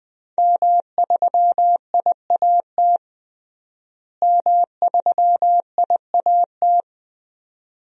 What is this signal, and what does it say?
Morse code, heard as a keyed tone, about 20 wpm, 700 Hz. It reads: M3IAT M3IAT
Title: DXCC-100 Call Signs - 5 Characters - Repeated 2x - 20wpm